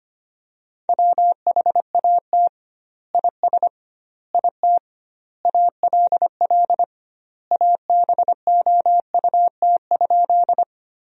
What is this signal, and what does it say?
Morse code, heard as a keyed tone, about 25 wpm, 700 Hz. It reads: WHAT IS IT ALL ABOUT?
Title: 2x Word-Spacing - Sentences from Top 100 Words - 25wpm